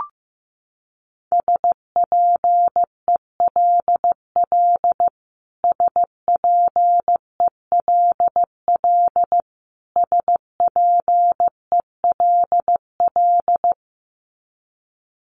Morse code, heard as a keyed tone, about 15 wpm, 700 Hz. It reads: SPELL SPELL SPELL
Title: Top 300 Words Repeated 15wpm